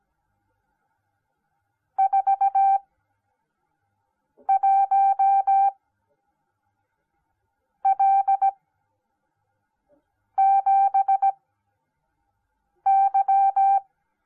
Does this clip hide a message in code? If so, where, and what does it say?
Yes: Morse code from 2.0 to 13.8 s, heard as a keyed tone, about 17 wpm, spaced as for 4 wpm, 780 Hz. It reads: 41L7Y